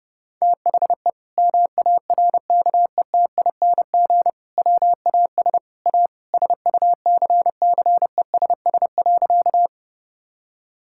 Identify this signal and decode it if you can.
Morse code, heard as a keyed tone, about 30 wpm, 700 Hz. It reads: THE MARKETING WAS A SUCCESS.
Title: Sentences from Top 1000 Words - Review Part 5 - 30wpm